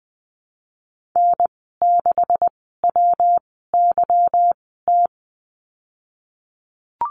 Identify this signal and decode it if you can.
Morse code, heard as a keyed tone, about 20 wpm, spaced as for 15 wpm, 700 Hz. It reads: N6WYT
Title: 147 Lesson 24g - Letter y - Call Signs 20wpm